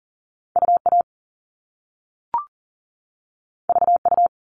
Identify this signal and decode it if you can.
Morse code, heard as a keyed tone, about 40 wpm, 700 Hz. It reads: UA VU
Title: All Cut Numbers - 2-Digit 40wpm